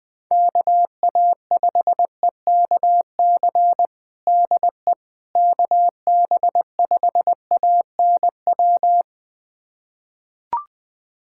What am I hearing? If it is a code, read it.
Morse code, heard as a keyed tone, about 20 wpm, 700 Hz. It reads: KA5EKC DE KB5ANW
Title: USA Callsign DE Callsign - 6 Characters - 20wpm